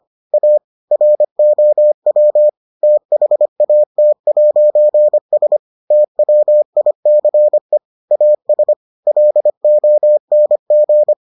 Morse code, heard as a keyed tone, about 25 wpm, 600 Hz. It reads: A ROW THAT'S TWICE AS LONG